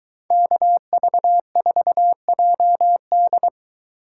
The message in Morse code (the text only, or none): KV4JD